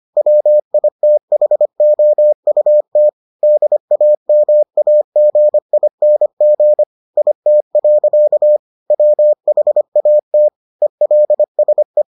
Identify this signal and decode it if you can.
Morse code, heard as a keyed tone, about 25 wpm, 600 Hz. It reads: WITHOUT DAMAGING IT. WHAT ELSE